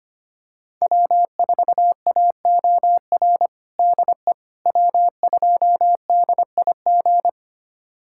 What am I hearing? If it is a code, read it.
Morse code, heard as a keyed tone, about 25 wpm, 700 Hz. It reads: W4AOR DE W2DIG
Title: USA Callsign DE Callsign - 5 Characters - 25wpm